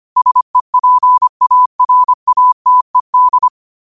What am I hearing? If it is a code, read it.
Morse code, heard as a keyed tone, about 25 wpm, 1000 Hz. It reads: SEPARATED